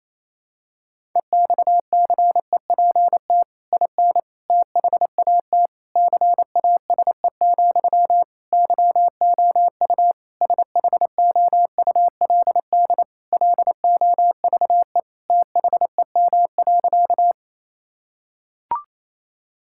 Morse code, heard as a keyed tone, about 28 wpm, 700 Hz. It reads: EXCEPT IN THAT CASE, YOU SHOULD LOVE THEM.
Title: Sentences from Top 1600 Words 28wpm